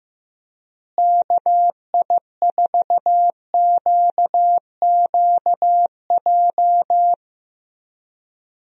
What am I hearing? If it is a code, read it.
Morse code, heard as a keyed tone, about 15 wpm, 700 Hz. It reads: KI4QQJ
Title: USA Call Signs - 6 Characters - Repeated 3x - 15wpm